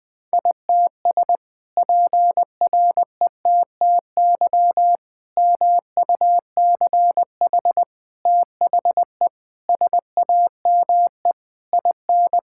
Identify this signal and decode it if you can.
Morse code, heard as a keyed tone, about 20 wpm, 700 Hz. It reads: ITS PRETTY MUCH THE SAME IN
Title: Sentences from Top 1000 Words - Review Part 6 - 20wpm